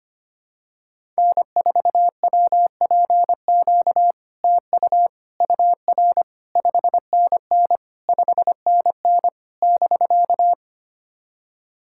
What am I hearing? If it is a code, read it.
Morse code, heard as a keyed tone, about 25 wpm, 700 Hz. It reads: N4WPQ TU UR 5NN 5NN <BK>
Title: SOTA QSO Practice 25wpm